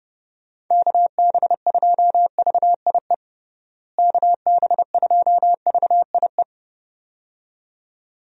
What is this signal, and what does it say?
Morse code, heard as a keyed tone, about 30 wpm, 700 Hz. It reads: KB2VIE KB2VIE
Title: USA Call Signs - 6 Characters - Repeated 2x - 30wpm